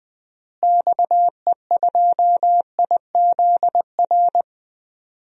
Morse code, heard as a keyed tone, about 21 wpm, 700 Hz. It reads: XE2IZR
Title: DXCC-100 Call Signs - 6 Characters - Repeated 2x - 20wpm